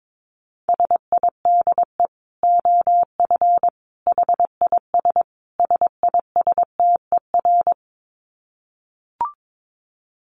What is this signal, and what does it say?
Morse code, heard as a keyed tone, about 22 wpm, 700 Hz. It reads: SIDE OF HIS SISTER
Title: Dit Challenge 22wpm